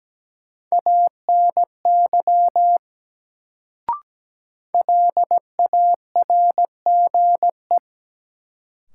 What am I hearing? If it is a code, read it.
Morse code, heard as a keyed tone, about 17 wpm, 700 Hz. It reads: ANY LARGE
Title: Sentences from Top 200 Words - New Word Drill - 17wpm